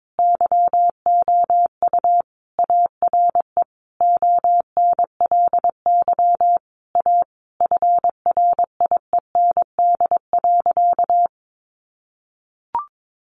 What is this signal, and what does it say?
Morse code, heard as a keyed tone, about 22 wpm, 700 Hz. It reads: YOU ARE ONLY A FRIEND.
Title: Sentences from Top 1000 Words 22wpm